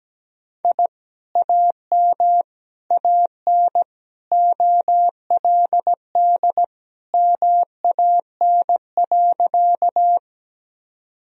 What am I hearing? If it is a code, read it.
Morse code, heard as a keyed tone, about 17 wpm, 700 Hz. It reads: I AM AN OLD MAN.
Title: Sentences from Top 1000 Words - Review Part 2 - 17wpm